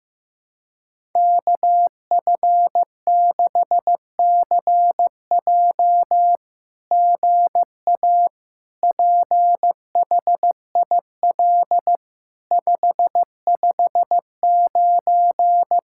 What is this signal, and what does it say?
Morse code, heard as a keyed tone, about 15 wpm, 700 Hz. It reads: KF6CJ GA PHIL 559
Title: SOTA QSO Practice 15wpm